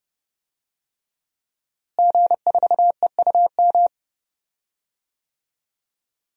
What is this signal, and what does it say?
Morse code, heard as a keyed tone, about 30 wpm, 700 Hz. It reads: G4EUM
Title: DXCC-100 Call Signs - 5 Characters - Repeated 3x - 30wpm